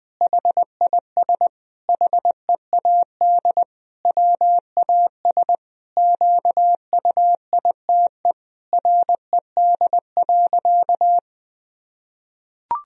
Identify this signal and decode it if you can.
Morse code, heard as a keyed tone, about 20 wpm, 700 Hz. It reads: HIS HEAD WAS QUITE RED.
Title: Sentences from Top 1000 Words - Review Part 2 - 20wpm